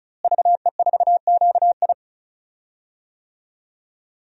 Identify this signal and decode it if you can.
Morse code, heard as a keyed tone, about 36 wpm, 700 Hz. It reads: VE4QI
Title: DXCC-100 Call Signs - 5 Characters - Repeated 3x - 35wpm